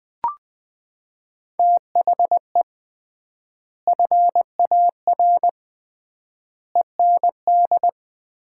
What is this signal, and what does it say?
Morse code, heard as a keyed tone, about 20 wpm, 700 Hz. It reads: THE FAR END
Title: 3x Word-Spacing - Sets of 5 Words - 20wpm